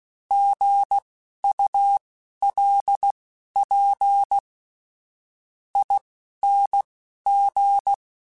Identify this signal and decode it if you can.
Morse code, heard as a keyed tone, about 16 wpm, spaced as for 10 wpm, 800 Hz. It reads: GULPING
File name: Alphabetical / Alphabetical mp3s / Gs @10